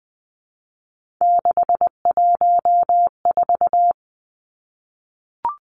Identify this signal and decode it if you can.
Morse code, heard as a keyed tone, about 20 wpm, 700 Hz. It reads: 614
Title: Three Challenging Letter-Number Combinations 20wpm